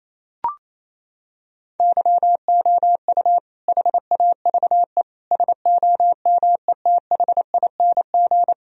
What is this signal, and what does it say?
Morse code, heard as a keyed tone, about 28 wpm, 700 Hz. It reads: YOU HAVE SOMETHING
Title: Sentences from Top 1000 Words - Review Part 4 - 28wpm